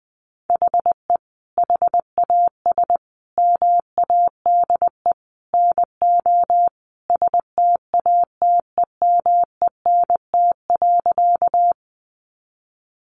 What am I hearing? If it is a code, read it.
Morse code, heard as a keyed tone, about 20 wpm, 700 Hz. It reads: HE HAS MADE NO STATEMENT.
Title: Sentences from Top 1200 Words 20wpm